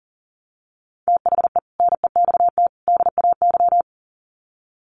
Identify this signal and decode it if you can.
Morse code, heard as a keyed tone, about 40 wpm, 700 Hz. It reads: THE NEXT DAY
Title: Sets of 3 Words 40wpm